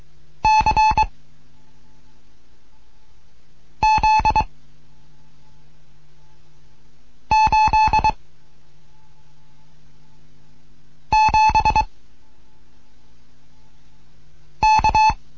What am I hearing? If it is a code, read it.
Morse code, heard as a keyed tone, about 23 wpm, 870 Hz. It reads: C Z 8 7 K